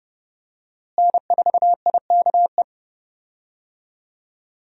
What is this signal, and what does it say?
Morse code, heard as a keyed tone, about 30 wpm, 700 Hz. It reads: N4IKE
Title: USA Call Signs - 5 Characters - Repeated 3x - 30wpm